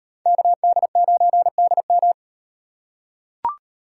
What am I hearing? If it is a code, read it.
Morse code, heard as a keyed tone, about 38 wpm, 700 Hz. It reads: KD9DM